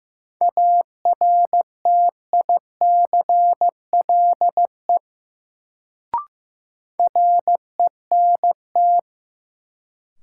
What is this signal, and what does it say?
Morse code, heard as a keyed tone, about 15 wpm, 700 Hz. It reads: ARTICLE RENT
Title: Sentences from Top 1000 Words - Review Part 5 - New Word Drill - 15wpm